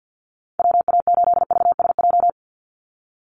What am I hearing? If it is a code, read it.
Morse code, heard as a keyed tone, about 50 wpm, 700 Hz. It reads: WA8VSJ
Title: Call Signs - 50wpm